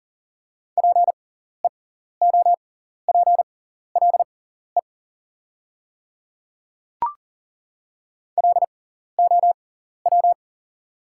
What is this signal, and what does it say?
Morse code, heard as a keyed tone, about 40 wpm, spaced as for 14 wpm, 690 Hz. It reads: PEOPLE LOW
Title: 6x Character Spacing - Top 200 Words - 40wpm